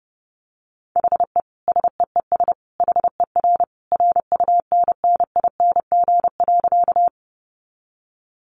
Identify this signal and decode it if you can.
Morse code, heard as a keyed tone, about 30 wpm, 700 Hz. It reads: HE SEES HER RUNNING.
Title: Additional Easy Phrases 30wpm